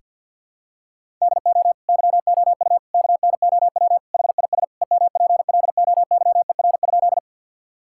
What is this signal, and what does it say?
Morse code, heard as a keyed tone, about 50 wpm, 700 Hz. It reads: DO YOU KNOW HIS EMPLOYER?
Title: Sentences from Top 1600 Words 50wpm